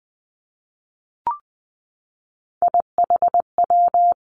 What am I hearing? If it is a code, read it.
Morse code, heard as a keyed tone, about 19 wpm, 700 Hz. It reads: IHW
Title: Three Challenging Letter-Number Combinations 20wpm